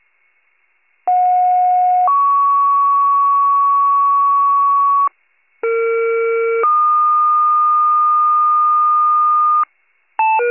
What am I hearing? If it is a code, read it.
Two-tone page, heard as a one-second tone, then a three-second tone, about 730 then 1120 Hz, about 460 then 1220 Hz. It is heard from 1.1 to 9.6 s.